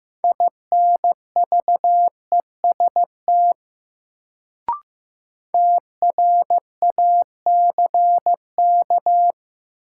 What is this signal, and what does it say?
Morse code, heard as a keyed tone, about 15 wpm, 700 Hz. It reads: INVEST TRACK
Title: Sentences from Top 1100 Words - New Word Drill - 15wpm